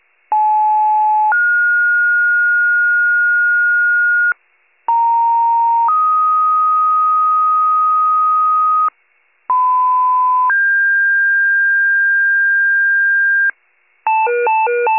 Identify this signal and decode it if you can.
Two-tone page, heard as a one-second tone, then a three-second tone, about 860 then 1430 Hz, about 930 then 1250 Hz, about 1010 then 1600 Hz, with noise. It is heard from 0.3 to 13.5 s.
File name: pager / 75 FIRE / wcalert